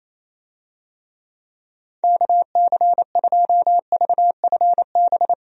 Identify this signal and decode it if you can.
Morse code, heard as a keyed tone, about 28 wpm, 700 Hz. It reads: KC2VFB
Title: USA Call Signs - 6 Characters - Repeated 2x - 28wpm